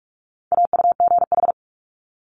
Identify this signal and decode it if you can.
Morse code, heard as a keyed tone, about 46 wpm, 700 Hz. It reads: AUGH
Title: Multi-letter Phonemes 45wpm